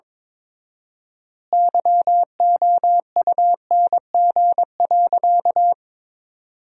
Morse code, heard as a keyed tone, about 22 wpm, 700 Hz. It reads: YOUNG.
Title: 4x Word-Spacing - Sentences from Top 400 Words - 22wpm